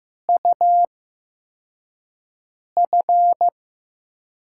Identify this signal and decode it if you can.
Morse code, heard as a keyed tone, about 15 wpm, spaced as for 4 wpm, 700 Hz. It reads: UF
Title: USA Call Signs - 6 Characters - 8x Character Spacing - 15wpm